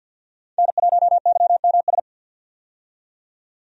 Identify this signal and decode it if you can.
Morse code, heard as a keyed tone, about 51 wpm, 700 Hz. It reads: N1YMS
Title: USA Call Signs - 5 Characters - 50wpm